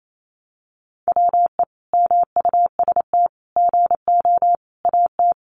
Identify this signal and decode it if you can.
Morse code, heard as a keyed tone, about 28 wpm, 700 Hz. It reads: WE MUST GO AT
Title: Sentences from Top 300 Words 28wpm